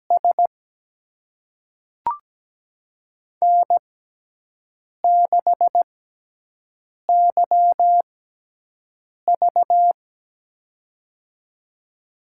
Morse code, heard as a keyed tone, about 17 wpm, spaced as for 6 wpm, 700 Hz. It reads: S N6YV
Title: USA Call Signs - 4 Characters - 6x Character Spacing - 17wpm